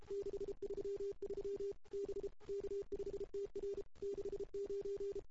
Morse code, heard as a keyed tone, about 33 wpm, 400 Hz. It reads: 633 B K5TR 69
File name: CW / 2008-ss-cw-k5tr-op-n5rz -03:00